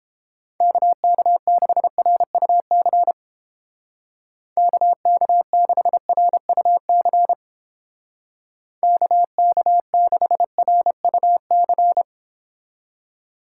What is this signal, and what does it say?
Morse code, heard as a keyed tone, about 29 wpm, 700 Hz. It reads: KK6RUC KK6RUC KK6RUC